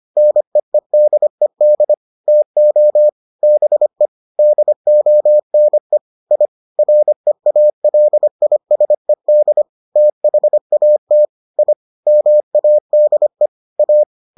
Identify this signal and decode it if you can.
Morse code, heard as a keyed tone, about 25 wpm, 600 Hz. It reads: NEEDED TO BE DONE I REALISED THAT I MADE A